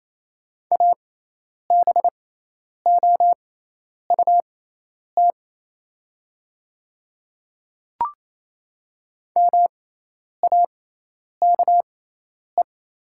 Morse code, heard as a keyed tone, about 28 wpm, spaced as for 10 wpm, 700 Hz. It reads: ABOUT MAKE
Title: 6x Character Spacing - Top 200 Words - 28wpm